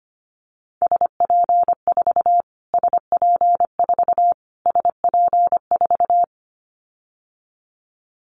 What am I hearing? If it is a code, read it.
Morse code, heard as a keyed tone, about 25 wpm, 700 Hz. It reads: SP4 SP4 SP4
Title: Callsign Prefixes for Top 25 DXCC - Repeated 3x - 25wpm